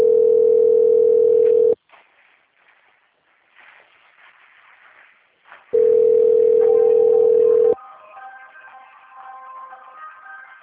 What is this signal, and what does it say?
Ringback tone (North America), 440 + 480 Hz.
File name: out- - - - - .10699